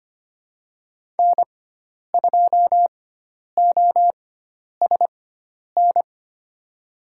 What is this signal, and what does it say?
Morse code, heard as a keyed tone, about 25 wpm, spaced as for 10 wpm, 700 Hz. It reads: N2OSN
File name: Lesson 16g - Letter c - Call Signs 25wpm